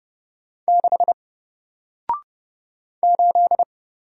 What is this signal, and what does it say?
Morse code, heard as a keyed tone, about 30 wpm, 700 Hz. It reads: 6 8
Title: Single Letter-Number Alt 30wpm